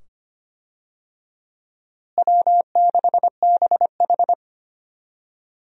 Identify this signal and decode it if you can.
Morse code, heard as a keyed tone, about 25 wpm, 700 Hz. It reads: W6BH
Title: USA Call Signs - 4 Characters - Repeated 2x - 25wpm